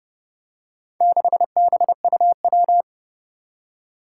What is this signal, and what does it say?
Morse code, heard as a keyed tone, about 30 wpm, 700 Hz. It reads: 6BUW
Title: Four Challenging Letter-Number Combinations 30wpm